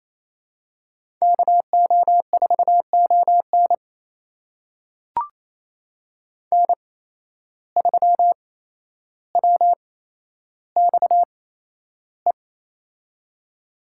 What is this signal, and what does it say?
Morse code, heard as a keyed tone, about 28 wpm, spaced as for 12 wpm, 700 Hz. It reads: KO4ON N3WXE
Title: USA Call Signs - 8x Character Spacing - 28wpm